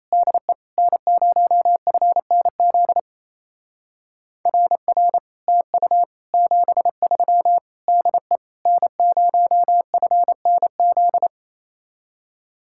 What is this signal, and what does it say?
Morse code, heard as a keyed tone, about 33 wpm, 700 Hz. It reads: DE N0FNZ RR TU 73 DE N0FNZ
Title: Speed-Racing Common QSO Phrases - 28wpm